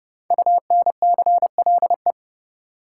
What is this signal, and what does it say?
Morse code, heard as a keyed tone, about 30 wpm, 700 Hz. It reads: UNCLE
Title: Five Letter Words - 30wpm